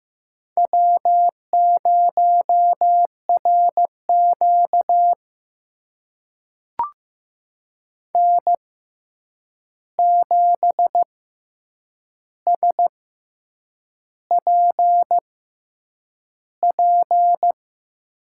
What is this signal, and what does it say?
Morse code, heard as a keyed tone, about 15 wpm, spaced as for 7 wpm, 700 Hz. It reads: W0RQ N7SPP
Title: USA Call Signs - 6x Character Spacing - 15wpm